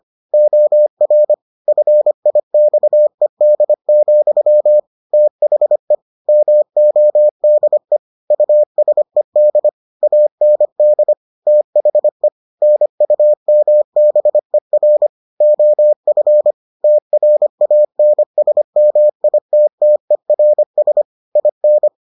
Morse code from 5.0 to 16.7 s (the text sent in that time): THE MODE USED AND THE NUMBER OF